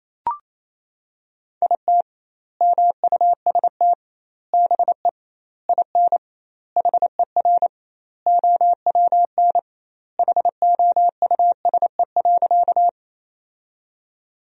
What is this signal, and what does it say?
Morse code, heard as a keyed tone, about 28 wpm, 700 Hz. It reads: IT MUST BE IN HER OWN HOUSE.
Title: 2x Word-Spacing - Sentences from Top 300 Words - 28wpm